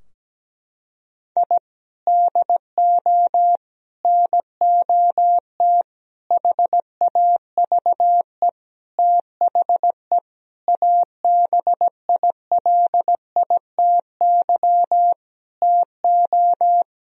Morse code, heard as a keyed tone, about 17 wpm, 700 Hz. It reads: I DO NOT HAVE THE ABILITY TO
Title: Sentences from Top 1000 Words - Review Part 3 - 17wpm